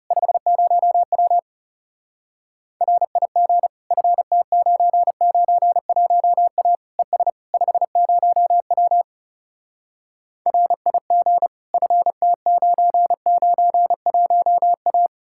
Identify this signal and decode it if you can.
Morse code, heard as a keyed tone, about 40 wpm, 700 Hz. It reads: 50W RIG FT991A ES 50W RIG FT991A